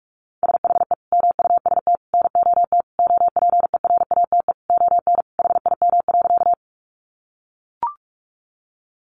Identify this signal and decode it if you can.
Morse code, heard as a keyed tone, about 44 wpm, 700 Hz. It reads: SHE MUST NOT OPERATE ON HIM.
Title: Sentences from Top 1500 Words 45wpm